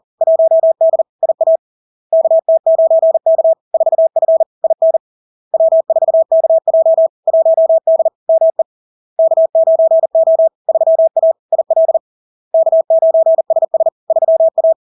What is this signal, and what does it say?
Morse code, heard as a keyed tone, about 40 wpm, 650 Hz. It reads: 1D IA KT9K 4F IN W4KJ 1D ME K9O 3A IL K9SS 3A
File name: LARK-FD-2022-Log-in-CW-40wpm-650-Hz